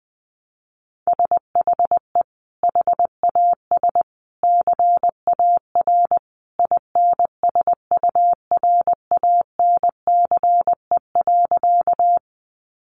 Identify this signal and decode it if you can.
Morse code, heard as a keyed tone, about 20 wpm, 700 Hz. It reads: SHE HAS CAR INSURANCE.
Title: Sentences from Top 1100 Words 20wpm